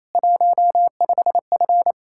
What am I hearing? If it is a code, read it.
Morse code, heard as a keyed tone, about 34 wpm, 700 Hz. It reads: 15F
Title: Speed-Racing Common QSO Phrases Encore - 28wpm